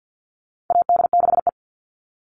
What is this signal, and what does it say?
Morse code, heard as a keyed tone, about 50 wpm, 700 Hz. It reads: AD6E
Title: Call Signs - 50wpm